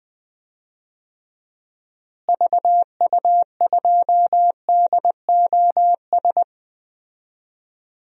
Morse code, heard as a keyed tone, about 20 wpm, 700 Hz. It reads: VU2DOS
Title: DXCC-100 Call Signs - 6 Characters - Repeated 2x - 20wpm